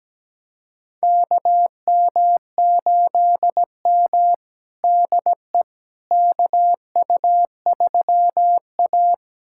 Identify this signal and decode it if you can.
Morse code, heard as a keyed tone, about 17 wpm, 700 Hz. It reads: KM8M DE KU3A
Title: USA Callsign DE Callsign - 4 Characters - 17wpm